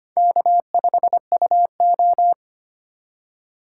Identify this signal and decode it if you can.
Morse code, heard as a keyed tone, about 31 wpm, 700 Hz. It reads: K5UO
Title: Speed Racing - USA 1x2 Call Signs 25wpm